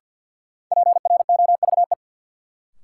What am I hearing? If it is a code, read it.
Morse code, heard as a keyed tone, about 53 wpm, 700 Hz. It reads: PROVE
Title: Sentences from Top 1600 Words - New Word Drill Alt - 50wpm